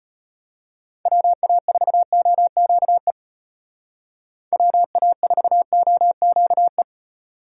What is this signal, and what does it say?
Morse code, heard as a keyed tone, about 30 wpm, 700 Hz. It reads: WA4OQE WA4OQE
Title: Speed Racing - USA Call Signs 25wpm